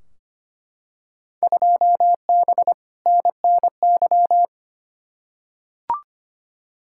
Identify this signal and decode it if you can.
Morse code, heard as a keyed tone, about 25 wpm, 700 Hz. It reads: 2B NNY
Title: Field Day Exchanges Encore - 25wpm